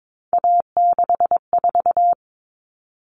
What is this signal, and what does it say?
Morse code, heard as a keyed tone, about 22 wpm, 700 Hz. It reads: A64
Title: Common Cut Numbers - 3-Digit 22wpm